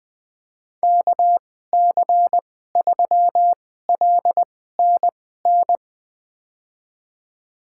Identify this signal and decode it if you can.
Morse code, heard as a keyed tone, about 20 wpm, spaced as for 15 wpm, 700 Hz. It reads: KC3LNN
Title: USA Call Signs - 6 Characters - 2x Character Spacing - 20wpm